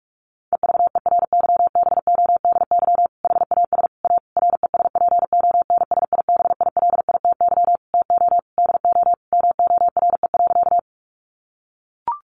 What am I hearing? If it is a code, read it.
Morse code, heard as a keyed tone, about 44 wpm, 700 Hz. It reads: EVERYBODY HAS A RESPONSIBILITY TO DO MORE.
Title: Sentences from Top 1400 Words 45wpm